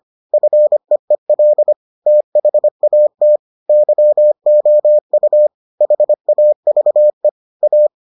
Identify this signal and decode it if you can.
Morse code, heard as a keyed tone, about 25 wpm, 600 Hz. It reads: FEEL THAT YOU HAVE A